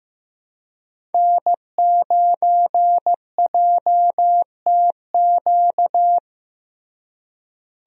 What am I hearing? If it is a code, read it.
Morse code, heard as a keyed tone, about 15 wpm, 700 Hz. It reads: N9JTQ
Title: USA Call Signs - 5 Characters - Repeated 3x - 15wpm